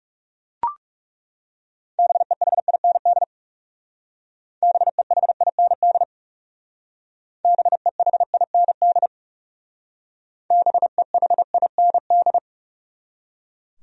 Morse code, heard as a keyed tone, about 38 wpm, 700 Hz. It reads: BEHIND BEHIND BEHIND BEHIND